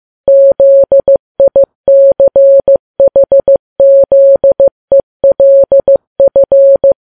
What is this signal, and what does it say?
Morse code, heard as a keyed tone, about 15 wpm, 550 Hz. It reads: ZICHZELF
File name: Files Web tls / Morse on9fv / morse15kps